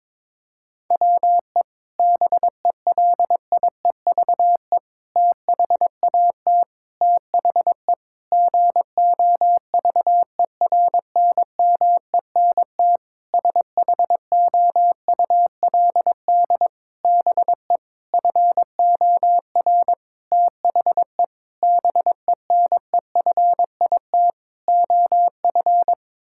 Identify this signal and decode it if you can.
Morse code, heard as a keyed tone, about 22 wpm, 700 Hz. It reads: WE BELIEVE THAT THE GOVERNMENT SHOULD BE FOR THE BENEFIT OF
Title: Sentences from Top 1000 Words - Review Part 6 - 22wpm